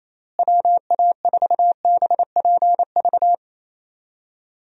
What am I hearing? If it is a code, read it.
Morse code, heard as a keyed tone, about 27 wpm, 700 Hz. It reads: WA4BPV
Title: USA Call Signs - 6 Characters - Repeated 3x - 28wpm